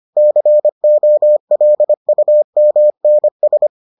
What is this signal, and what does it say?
Morse code, heard as a keyed tone, about 25 wpm, 600 Hz. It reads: COLUMNS